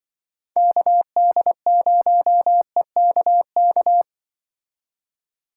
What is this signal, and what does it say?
Morse code, heard as a keyed tone, about 24 wpm, 700 Hz. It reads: KD0EKK
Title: Speed Racing - USA Call Signs 20wpm